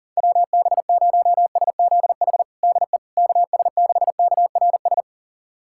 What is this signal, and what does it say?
Morse code, heard as a keyed tone, about 40 wpm, 700 Hz. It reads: WB0SZH DE KS6KRS
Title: USA Callsign DE Callsign 40wpm